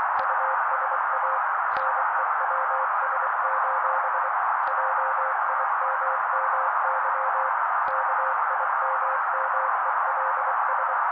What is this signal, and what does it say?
Morse code, heard as a keyed tone, about 24 wpm, 530 Hz. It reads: USA NEWS: JIMMY KIMMELS
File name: daily / 23wpm noise